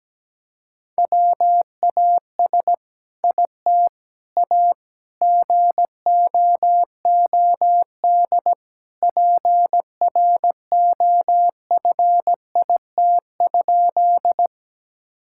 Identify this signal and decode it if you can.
Morse code, heard as a keyed tone, about 17 wpm, 700 Hz. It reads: WAS IT A GOOD PROFIT?
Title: Sentences from Top 1000 Words - Review Part 5 - 17wpm